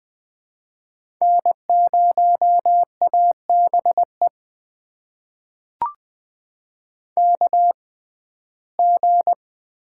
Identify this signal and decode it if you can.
Morse code, heard as a keyed tone, about 20 wpm, spaced as for 14 wpm, 700 Hz. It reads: N0ABE KG
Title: USA Call Signs - 5 Characters - 6x Character Spacing - 20wpm